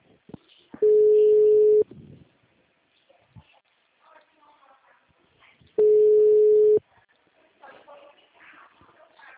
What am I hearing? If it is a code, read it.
Ringback tone (Europe, 425 Hz), with speech.